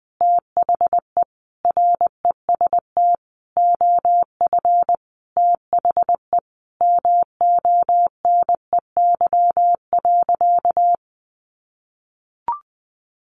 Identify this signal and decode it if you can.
Morse code, heard as a keyed tone, about 20 wpm, 700 Hz. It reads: THE REST OF THE MONEY.
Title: Sentences from Top 1500 Words 20wpm